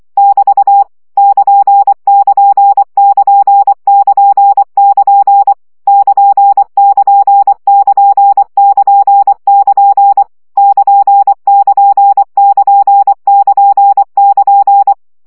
Morse code, heard as a keyed tone, about 24 wpm, 800 Hz. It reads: <BT> <KN><KN><KN><KN><KN> <KN><KN><KN><KN><KN> <KN><KN><KN><KN><KN>
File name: CW gyakorlas nyito 20 baud